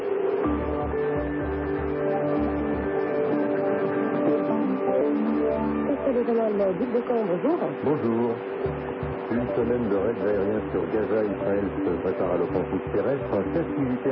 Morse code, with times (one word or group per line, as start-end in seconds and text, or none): none